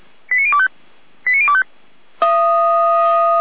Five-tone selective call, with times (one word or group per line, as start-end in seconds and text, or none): 0.3-0.7 89025
1.3-1.6 89025